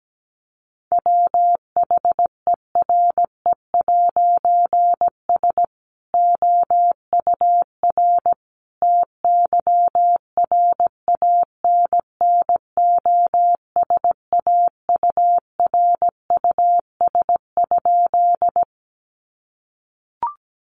Morse code, heard as a keyed tone, about 17 wpm, 700 Hz. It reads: WHERE'S OUR TYRANNOSAURUS?
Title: A Sound Of Thunder 17wpm